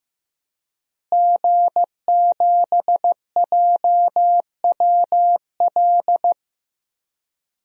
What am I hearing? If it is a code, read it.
Morse code, heard as a keyed tone, about 15 wpm, 700 Hz. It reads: G7JWL
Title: DXCC-100 Call Signs - 5 Characters - Repeated 3x - 15wpm